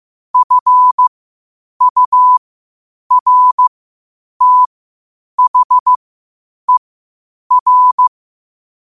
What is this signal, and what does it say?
Morse code, heard as a keyed tone, about 15 wpm, spaced as for 8 wpm, 1000 Hz. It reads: FURTHER
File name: qotdfiles / 8wpm